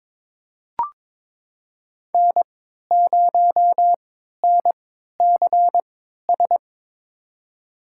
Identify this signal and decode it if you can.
Morse code, heard as a keyed tone, about 22 wpm, spaced as for 13 wpm, 700 Hz. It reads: N0NCS
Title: USA Call Signs - 5 Characters - 3x Character Spacing - 22wpm